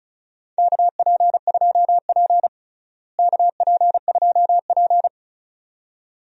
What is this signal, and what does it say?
Morse code, heard as a keyed tone, about 36 wpm, 700 Hz. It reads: KP2P KP2P
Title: DXCC-100 Call Signs - 4 Characters - Repeated 2x - 35wpm